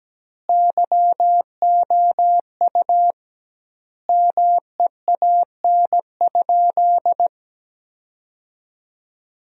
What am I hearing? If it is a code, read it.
Morse code, heard as a keyed tone, about 17 wpm, 700 Hz. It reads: YOU MEAN?
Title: 2x Word-Spacing - Sentences from Top 200 Words - 17wpm